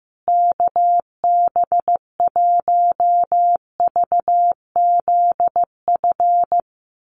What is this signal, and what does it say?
Morse code, heard as a keyed tone, about 15 wpm, 700 Hz. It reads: KB1VZF
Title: USA Call Signs Repeated 2x 15wpm